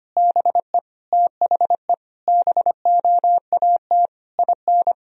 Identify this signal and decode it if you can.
Morse code, heard as a keyed tone, about 25 wpm, 700 Hz. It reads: BE THE BOAT IN
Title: Sentences from Top 1000 Words - Review Part 3 - 25wpm